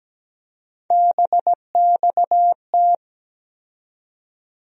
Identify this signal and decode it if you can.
Morse code, heard as a keyed tone, about 17 wpm, 700 Hz. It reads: BXT
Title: USA Call Sign Postfixes - 3 Letters 17wpm